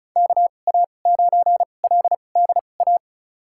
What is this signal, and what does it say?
Morse code, heard as a keyed tone, about 36 wpm, spaced as for 25 wpm, 700 Hz. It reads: KA9LDA
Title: USA Call Signs - 6 Characters - 2x Character Spacing - 35wpm